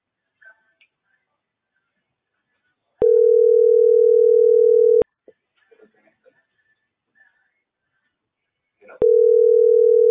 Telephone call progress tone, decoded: ringback tone

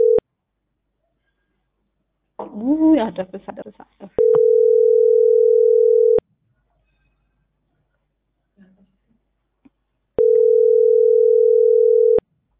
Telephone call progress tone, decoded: ringback tone

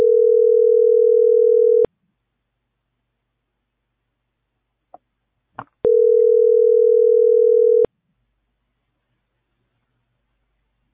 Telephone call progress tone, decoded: ringback tone

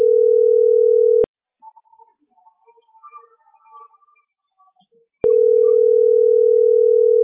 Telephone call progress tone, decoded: ringback tone

